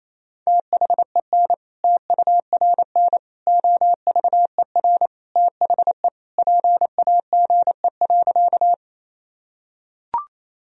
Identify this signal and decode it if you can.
Morse code, heard as a keyed tone, about 28 wpm, 700 Hz. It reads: THEN TURN OVER THE PAGE.
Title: Sentences from Top 1000 Words - Review Part 1 - 28wpm